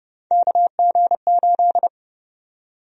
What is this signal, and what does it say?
Morse code, heard as a keyed tone, about 30 wpm, 700 Hz. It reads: KG8
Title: USA Call Sign Prefixes - 30wpm